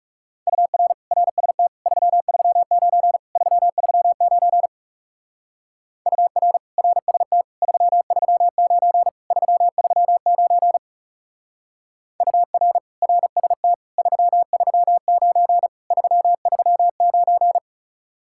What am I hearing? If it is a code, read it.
Morse code, heard as a keyed tone, about 35 wpm, 700 Hz. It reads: UR RST 339 339 UR RST 339 339 UR RST 339 339
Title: Speed-Racing Common QSO Phrases - 30wpm